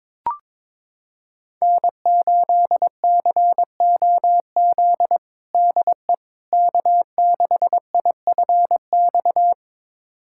Morse code, heard as a keyed tone, about 22 wpm, 700 Hz. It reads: N8COZ DE K6IFX
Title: USA Callsign DE Callsign - 5 Characters - 22wpm